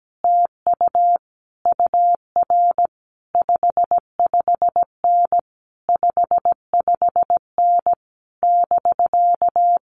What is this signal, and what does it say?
Morse code, heard as a keyed tone, about 17 wpm, 700 Hz. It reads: TU UR 55N 55N <BK>
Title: Common QSO Phrases Encore 17wpm